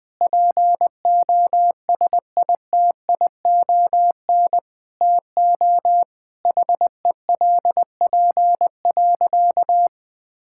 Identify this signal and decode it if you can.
Morse code, heard as a keyed tone, about 20 wpm, 700 Hz. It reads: POSITION TO HELP.
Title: Sentences with 7-Word Phrases - 20wpm